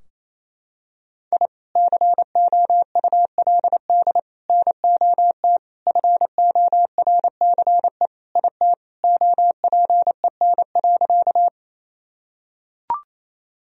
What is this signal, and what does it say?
Morse code, heard as a keyed tone, about 27 wpm, 700 Hz. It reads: I COULD NOT FORCE IT OPEN.
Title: Sentences from Top 1000 Words - Review Part 3 - 28wpm